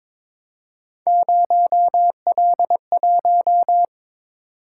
Morse code, heard as a keyed tone, about 22 wpm, 700 Hz. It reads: 0L1
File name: practice / 22wpm / Three Letter-Number Combinations - 22wpm